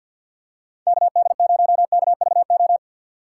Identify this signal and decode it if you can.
Morse code, heard as a keyed tone, about 52 wpm, 700 Hz. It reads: KD0KVO
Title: USA Call Signs - 3x Character Spacing - 50wpm